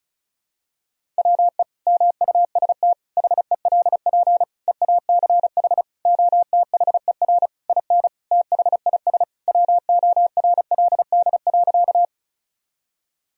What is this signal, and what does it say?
Morse code, heard as a keyed tone, about 36 wpm, 700 Hz. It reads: WE MUST HELP EACH OTHER IN THIS WORLD.
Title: Sentences from Top 1000 Words - Review Part 1 - 35wpm